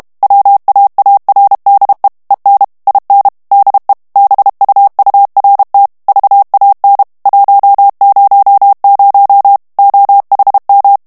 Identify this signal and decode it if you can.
Morse code, heard as a keyed tone, about 32 wpm, 800 Hz. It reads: WAARDE EN IN DE BUURT VAN 100 OHM
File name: HOLLANDAIS texte clair 1 vitesse 30 mots minute